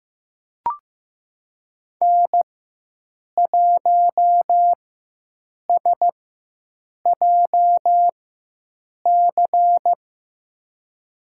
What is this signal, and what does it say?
Morse code, heard as a keyed tone, about 15 wpm, spaced as for 7 wpm, 700 Hz. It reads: N1SJC
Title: USA Call Signs - 5 Characters - 4x Character Spacing - 15wpm